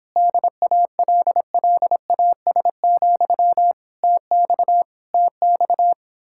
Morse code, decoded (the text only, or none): DALLAS, TX TX